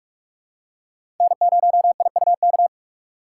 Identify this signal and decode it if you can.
Morse code, heard as a keyed tone, about 46 wpm, 700 Hz. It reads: N0IUK
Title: USA Call Signs - 5 Characters - 2x Character Spacing - 45wpm